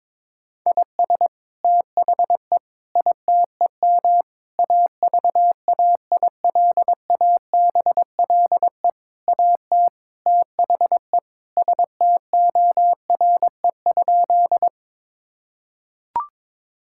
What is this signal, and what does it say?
Morse code, heard as a keyed tone, about 22 wpm, 700 Hz. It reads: IS THE ITEM AVAILABLE AT THE STORE?
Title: Sentences from Top 1000 Words - Review Part 6 - 22wpm